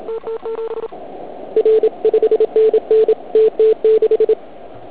Morse code, heard as a keyed tone, about 36 wpm, 440 Hz, with radio noise. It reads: TT7 R 5NN TT6
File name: Archiv / CUC106 / cuc106 PILEUP